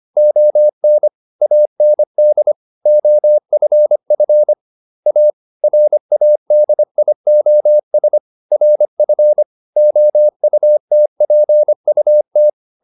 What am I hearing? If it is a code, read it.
Morse code, heard as a keyed tone, about 25 wpm, 600 Hz. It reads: ON AND OFF A RADIOS RF OUTPUT